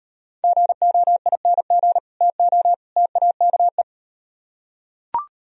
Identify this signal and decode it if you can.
Morse code, heard as a keyed tone, about 38 wpm, 700 Hz. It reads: GOING TO TAKE